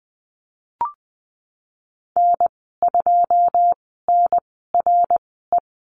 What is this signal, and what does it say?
Morse code, heard as a keyed tone, about 20 wpm, spaced as for 15 wpm, 700 Hz. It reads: N2NRE
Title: 082 Lesson 14g - Number 2 - Call Signs 20wpm